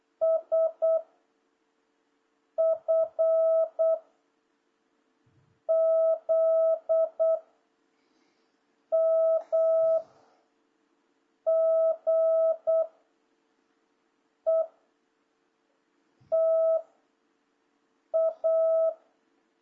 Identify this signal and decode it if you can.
Morse code, heard as a keyed tone, about 8 wpm, 650 Hz. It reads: S F Z M G E T A
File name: jh8cbh / 2026cw